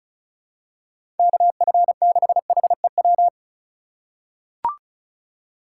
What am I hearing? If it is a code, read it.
Morse code, heard as a keyed tone, about 34 wpm, 700 Hz. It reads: KF6HEW
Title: USA Call Signs - 3x Character Spacing - 35wpm